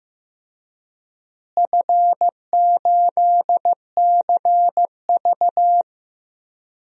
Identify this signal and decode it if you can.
Morse code, heard as a keyed tone, about 16 wpm, 700 Hz. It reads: F8CV
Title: DXCC-100 Call Signs - 4 Characters - Repeated 3x - 15wpm